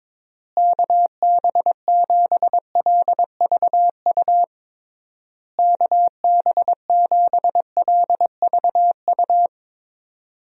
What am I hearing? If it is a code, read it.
Morse code, heard as a keyed tone, about 22 wpm, 700 Hz. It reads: KB7LVU KB7LVU
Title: USA Call Signs - 6 Characters - Repeated 3x - 22wpm